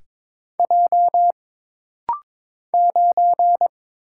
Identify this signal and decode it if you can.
Morse code, heard as a keyed tone, about 22 wpm, 700 Hz. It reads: J 9
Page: This rapid-fire timing is geared to push your ICR (Instant Character Recognition) skills.